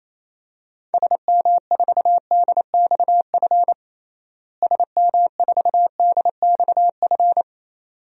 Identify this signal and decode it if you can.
Morse code, heard as a keyed tone, about 27 wpm, 700 Hz. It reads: SM4DXF SM4DXF
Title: DXCC-100 Call Signs - 6 Characters - Repeated 2x - 28wpm